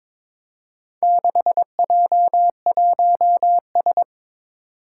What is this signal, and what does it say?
Morse code, heard as a keyed tone, about 22 wpm, 700 Hz. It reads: 6J1S
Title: Four Challenging Letter-Number Combinations 22wpm